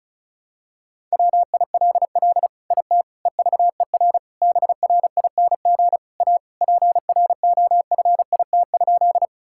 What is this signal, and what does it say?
Morse code, heard as a keyed tone, about 34 wpm, 700 Hz. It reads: WILL IT EVER BRING A PROFIT?
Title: Sentences from Top 1000 Words - Review Part 5 - 35wpm